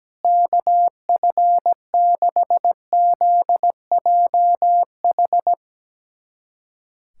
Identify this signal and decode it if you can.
Morse code, heard as a keyed tone, about 17 wpm, 700 Hz. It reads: KF6ZJH
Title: USA Callsign DE Callsign - 6 Characters - 17wpm